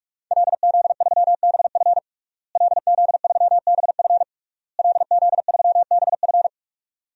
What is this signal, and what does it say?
Morse code, heard as a keyed tone, about 46 wpm, 700 Hz. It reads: LZ3BF LZ3BF LZ3BF
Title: DXCC-100 Call Signs - 5 Characters - Repeated 3x - 45wpm